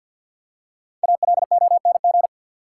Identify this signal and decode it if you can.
Morse code, heard as a keyed tone, about 50 wpm, 700 Hz. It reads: ALONG
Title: Five Letter Words - 50wpm